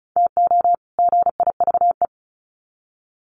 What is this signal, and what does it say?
Morse code, heard as a keyed tone, about 34 wpm, 700 Hz. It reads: TO GIVE
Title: Sets of 2 Words 35wpm